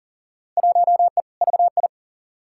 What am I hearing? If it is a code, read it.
Morse code, heard as a keyed tone, about 40 wpm, 700 Hz. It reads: 1E VI
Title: Field Day Exchanges Encore - 40wpm